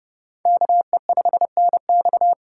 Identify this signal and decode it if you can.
Morse code, heard as a keyed tone, about 29 wpm, 700 Hz. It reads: KE5NX